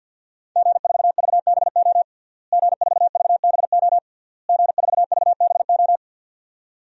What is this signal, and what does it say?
Morse code, heard as a keyed tone, about 46 wpm, 700 Hz. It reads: G4VBO G4VBO G4VBO
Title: DXCC-100 Call Signs - 5 Characters - Repeated 3x - 50wpm